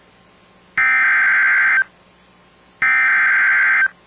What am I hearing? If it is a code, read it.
Emergency Alert System (SAME) tones (header bursts), heard as harsh buzzing bursts. It is heard from 0.8 to 3.8 s.